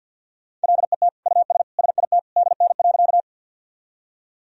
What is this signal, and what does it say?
Morse code, heard as a keyed tone, about 51 wpm, 700 Hz. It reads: LET US SIT DN.
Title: Sentences from Top 500 Words and Abbreviations 50wpm